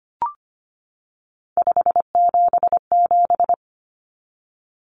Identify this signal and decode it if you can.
Morse code, heard as a keyed tone, about 25 wpm, 700 Hz. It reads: 577
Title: Common Cut Numbers - 3-Digit 25wpm